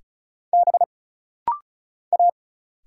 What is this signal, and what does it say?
Morse code, heard as a keyed tone, about 34 wpm, 700 Hz. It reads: B A
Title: Single Letter-Number Alt - Rapid-Fire 35wpm